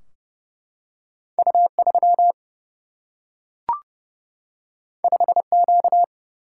Morse code, heard as a keyed tone, about 30 wpm, 700 Hz. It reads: U 3 5 Q
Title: Single-Letter Single-Number 30wpm